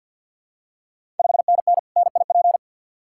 Morse code, heard as a keyed tone, about 51 wpm, 700 Hz. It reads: 5NN NIP
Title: Ontario QSO Party Exchange 50wpm